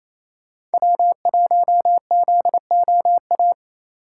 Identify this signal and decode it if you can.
Morse code, heard as a keyed tone, about 36 wpm, 700 Hz. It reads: W1ZOA